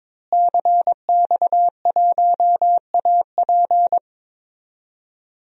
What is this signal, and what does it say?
Morse code, heard as a keyed tone, about 22 wpm, 700 Hz. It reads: CX1AP
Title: DXCC-100 Call Signs - 5 Characters - Repeated 2x - 22wpm